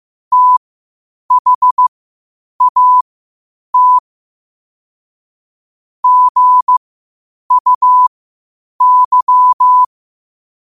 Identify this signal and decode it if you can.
Morse code, heard as a keyed tone, about 15 wpm, spaced as for 8 wpm, 1000 Hz. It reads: THAT GUY
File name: qotdfiles / 8wpm